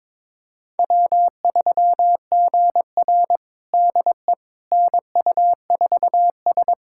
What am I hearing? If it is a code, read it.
Morse code, heard as a keyed tone, about 22 wpm, 700 Hz. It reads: W3GR DE NU4S
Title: USA Callsign DE Callsign - 4 Characters - 22wpm